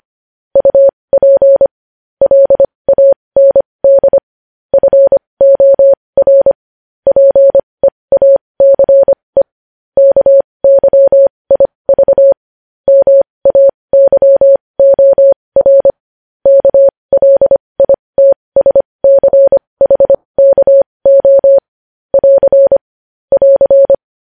Morse code, heard as a keyed tone, about 25 wpm, spaced as for 20 wpm, 550 Hz. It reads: UP LAND FOR PEACE KYIV MAYOR KLITSCHKO <AR> <AR>
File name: podcast / News headlines-1x-25x20